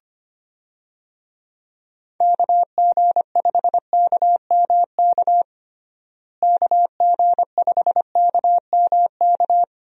Morse code, heard as a keyed tone, about 25 wpm, 700 Hz. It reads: KG5KMK KG5KMK
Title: USA Call Signs - 6 Characters - Repeated 2x - 25wpm